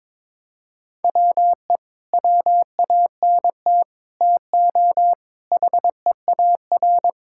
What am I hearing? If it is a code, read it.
Morse code, heard as a keyed tone, about 22 wpm, 700 Hz. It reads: WE WANT TO HEAR
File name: Sentences with 9-Word Phrases - 22wpm